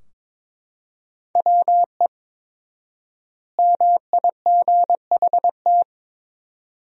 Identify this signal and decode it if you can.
Morse code, heard as a keyed tone, about 22 wpm, 700 Hz. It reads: WE MIGHT
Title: 4x Word-Spacing - Sentences from Top 300 Words - 22wpm